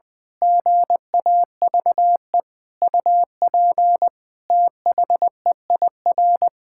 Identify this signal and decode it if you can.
Morse code, heard as a keyed tone, about 20 wpm, 700 Hz. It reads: GAVE UP THEIR
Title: Pangram Challenge v2 - 20wpm